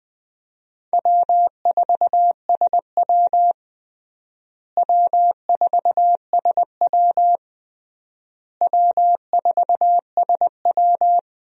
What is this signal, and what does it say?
Morse code, heard as a keyed tone, about 21 wpm, 700 Hz. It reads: W4SW W4SW W4SW
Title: USA Call Signs - 4 Characters - Repeated 3x - 20wpm